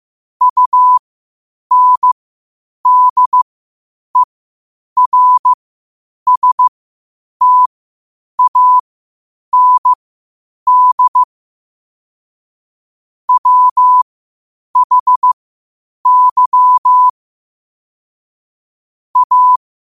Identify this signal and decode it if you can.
Morse code, heard as a keyed tone, about 15 wpm, spaced as for 8 wpm, 1000 Hz. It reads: UNDERSTAND WHY A
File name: qotdfiles / 8wpm